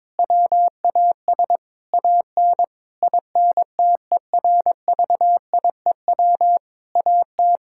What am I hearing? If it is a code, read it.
Morse code, heard as a keyed tone, about 22 wpm, 700 Hz. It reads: WAS AN INTERVIEW AT
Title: Sentences from Top 1600 Words 22wpm